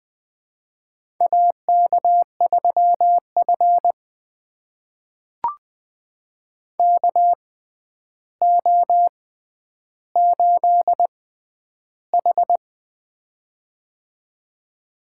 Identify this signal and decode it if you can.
Morse code, heard as a keyed tone, about 20 wpm, spaced as for 10 wpm, 700 Hz. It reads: AK3F KO8H
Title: USA Call Signs - 6x Character Spacing - 20wpm